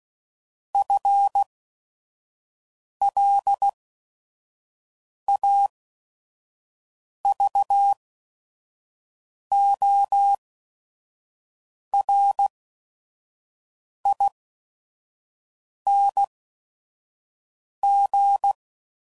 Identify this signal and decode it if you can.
Morse code, heard as a keyed tone, about 16 wpm, spaced as for 5 wpm, 800 Hz. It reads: FLAVORING
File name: Suffixes / Special ING's / VerySlowToFaster ING's